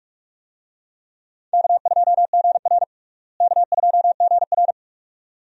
Morse code, heard as a keyed tone, about 42 wpm, 700 Hz. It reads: K2GR K2GR
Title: USA Call Signs - 4 Characters - Repeated 2x - 45wpm